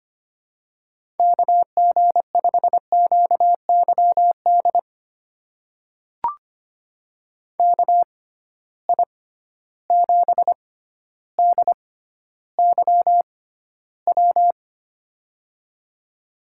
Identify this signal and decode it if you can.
Morse code, heard as a keyed tone, about 25 wpm, spaced as for 13 wpm, 700 Hz. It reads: KG5QYD KI7DYW
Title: USA Call Signs - 6 Characters - 6x Character Spacing - 25wpm